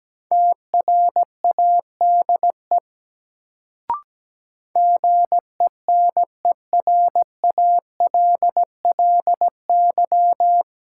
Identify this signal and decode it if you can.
Morse code, heard as a keyed tone, about 17 wpm, 700 Hz. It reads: TRADE GENERALLY
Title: Sentences from Top 600 Words - New Word Drill - 17wpm